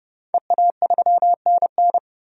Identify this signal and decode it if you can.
Morse code, heard as a keyed tone, about 30 wpm, 700 Hz. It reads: EA3NN
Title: DXCC-100 Call Signs - 5 Characters - Repeated 2x - 30wpm